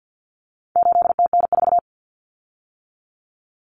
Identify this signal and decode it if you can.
Morse code, heard as a keyed tone, about 50 wpm, 700 Hz. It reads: 8TN4
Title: Common Cut Numbers - 4-Digit 50wpm